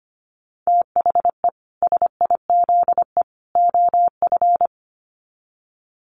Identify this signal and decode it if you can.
Morse code, heard as a keyed tone, about 25 wpm, 700 Hz. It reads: THE SIZE OF
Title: Sets of 3 Words 25wpm